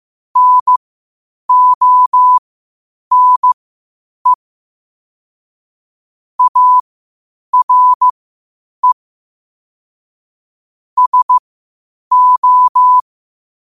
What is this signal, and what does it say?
Morse code, heard as a keyed tone, about 15 wpm, spaced as for 8 wpm, 1000 Hz. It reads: NONE ARE SO
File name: qotdfiles / 8wpm